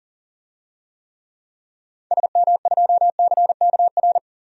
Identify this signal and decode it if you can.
Morse code, heard as a keyed tone, about 40 wpm, 700 Hz. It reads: SM2CKR
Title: DXCC-100 Call Signs - 6 Characters - 40wpm